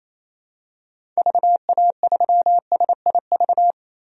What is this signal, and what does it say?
Morse code, heard as a keyed tone, about 27 wpm, 700 Hz. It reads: VA3SIV
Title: DXCC-100 Call Signs - 6 Characters - Repeated 2x - 28wpm